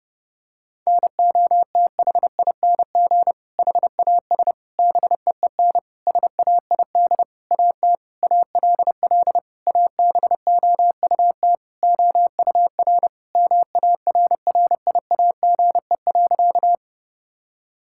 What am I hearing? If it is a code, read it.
Morse code, heard as a keyed tone, about 30 wpm, 700 Hz. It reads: NOTHING HAS BEEN SAID AT ALL ABOUT OUR MARRIAGE.
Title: Sentences from Top 1000 Words - Review Part 5 - 30wpm